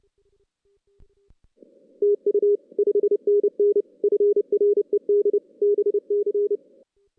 Morse code, heard as a keyed tone, about 30 wpm, 410 Hz. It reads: TU 5NN FRED BC
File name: 2016 JAN CW NAQP / 2016 Jan naqp k5ot - 0-6